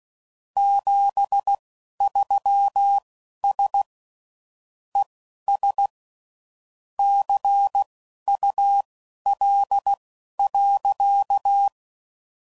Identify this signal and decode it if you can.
Morse code, heard as a keyed tone, about 16 wpm, spaced as for 11 wpm, 800 Hz. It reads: 73S ES CUL.
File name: CW Code Groups @10WPM / QSOs / QSO kc7haa